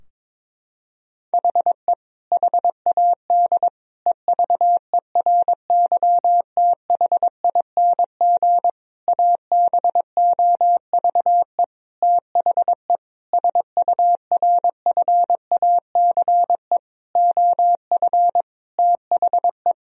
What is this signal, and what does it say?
Morse code, heard as a keyed tone, about 22 wpm, 700 Hz. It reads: HE HAD EVERYTHING ABOVE THE SURFACE OF THE
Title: Sentences from Top 1000 Words - Review Part 3 - 22wpm